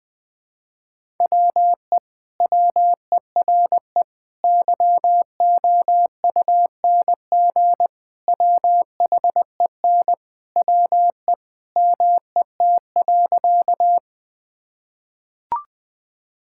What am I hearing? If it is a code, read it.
Morse code, heard as a keyed tone, about 21 wpm, 700 Hz. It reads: WE WERE YOUNG WHEN WE MET.
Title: Sentences with 2-Word Phrases - 20wpm